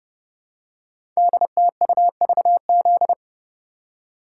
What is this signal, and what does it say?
Morse code, heard as a keyed tone, about 30 wpm, 700 Hz. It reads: DTUVZ
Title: Code Groups - 5 Letters - 30wpm